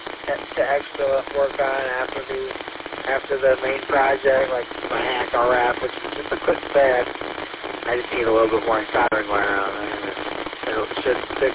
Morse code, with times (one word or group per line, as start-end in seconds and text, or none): none